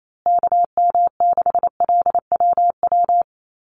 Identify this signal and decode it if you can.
Morse code, heard as a keyed tone, about 28 wpm, 700 Hz. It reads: KM6LWW
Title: USA Call Signs Repeated 2x 28wpm